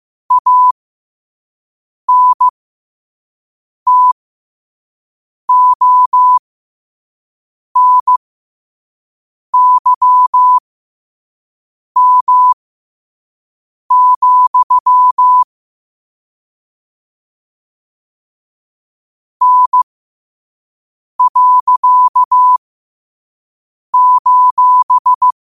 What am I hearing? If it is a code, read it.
Morse code, heard as a keyed tone, about 15 wpm, spaced as for 5 wpm, 1000 Hz. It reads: ANTONYM, N.:
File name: qotdfiles / 5wpm